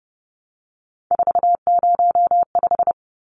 Morse code, heard as a keyed tone, about 30 wpm, 700 Hz. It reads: 405
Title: Numbers - 3-Digit 30wpm